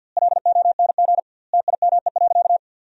Morse code, heard as a keyed tone, about 50 wpm, 700 Hz. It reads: LONG TIME.
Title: 2x Word-Spacing - Sentences from Top 400 Words - 50wpm